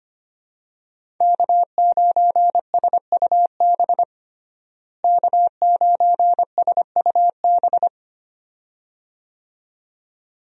Morse code, heard as a keyed tone, about 25 wpm, 700 Hz. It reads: K9SUB K9SUB
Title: USA Call Signs - 5 Characters - Repeated 2x - 25wpm